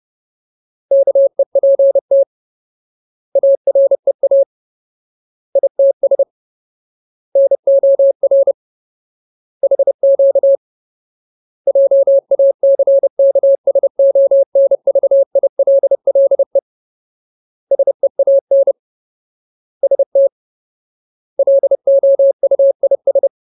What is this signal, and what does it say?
Morse code, heard as a keyed tone, about 30 wpm, 550 Hz. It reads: KEPT AREA ITS NOR HQ JACKSONVILLE SEAN ST LOUIS
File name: Ultimate Word List / 30WPM Ultimate